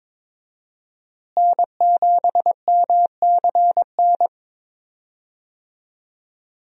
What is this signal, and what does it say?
Morse code, heard as a keyed tone, about 22 wpm, 700 Hz. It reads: N7MCN
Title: USA Call Signs - 5 Characters - Repeated 2x - 22wpm